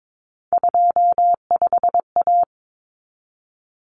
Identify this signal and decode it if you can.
Morse code, heard as a keyed tone, about 22 wpm, 700 Hz. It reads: 25A
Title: Common Cut Numbers - 3-Digit 22wpm